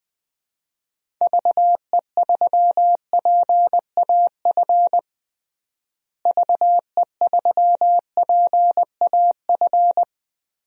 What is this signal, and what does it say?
Morse code, heard as a keyed tone, about 20 wpm, 700 Hz. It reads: VE3PAF VE3PAF
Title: DXCC-100 Call Signs - 6 Characters - Repeated 3x - 20wpm